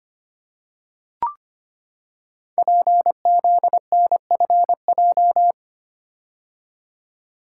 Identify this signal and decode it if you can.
Morse code, heard as a keyed tone, about 25 wpm, 700 Hz. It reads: PZNFJ
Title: Code Groups - 5 Letters - 25wpm